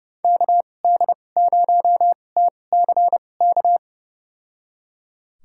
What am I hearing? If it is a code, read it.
Morse code, heard as a keyed tone, about 31 wpm, spaced as for 22 wpm, 700 Hz. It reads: KD0TCK
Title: USA Call Signs - 6 Characters - 2x Character Spacing - 30wpm